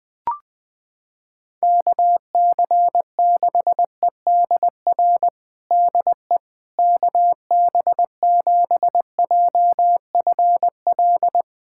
Morse code, heard as a keyed tone, about 20 wpm, 700 Hz. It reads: KC6EDR DE KB7JFL
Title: USA Callsign DE Callsign 20wpm